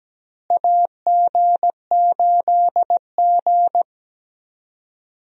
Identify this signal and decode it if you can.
Morse code, heard as a keyed tone, about 17 wpm, 700 Hz. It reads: AG8G
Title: USA Call Signs - 4 Characters - Repeated 3x - 17wpm